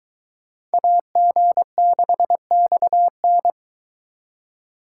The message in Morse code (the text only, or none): AG6XN